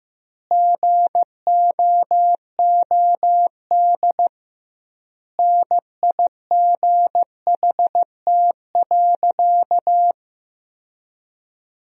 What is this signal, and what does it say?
Morse code, heard as a keyed tone, about 15 wpm, 700 Hz. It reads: GOOD NIGHT.
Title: 2x Word-Spacing - Sentences from Top 300 Words - 15wpm